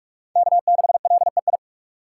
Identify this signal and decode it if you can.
Morse code, heard as a keyed tone, about 43 wpm, 700 Hz. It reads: K6LEI
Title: USA Call Signs - 5 Characters - 4x Character Spacing - 45wpm